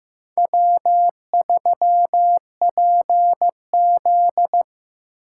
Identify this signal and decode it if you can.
Morse code, heard as a keyed tone, about 18 wpm, 700 Hz. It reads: W3PZ
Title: Speed Racing - USA 1x2 Call Signs 15wpm